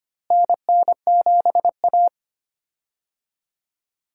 Morse code, heard as a keyed tone, about 25 wpm, 700 Hz. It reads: NN7A
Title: USA Call Signs - 4 Characters - Repeated 2x - 25wpm